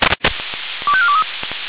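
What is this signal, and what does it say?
Five-tone selective call (ZVEI), heard as quick runs of five tones, tones of about 70 ms. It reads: 25423
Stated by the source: ZVEI five-tone selective call received on an SDR.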